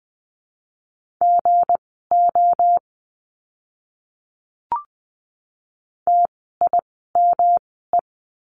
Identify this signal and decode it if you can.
Morse code, heard as a keyed tone, about 20 wpm, spaced as for 14 wpm, 700 Hz. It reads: GO TIME
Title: 192 Lesson 31f - Letter v - Words 20wpm